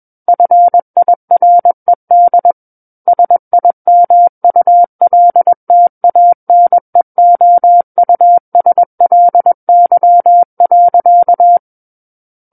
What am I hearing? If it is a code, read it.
Morse code, heard as a keyed tone, about 21 wpm, 700 Hz. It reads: FIRED SIMULTANEOUSLY.
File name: Book One / 12-Chapter 12 - 21 WPM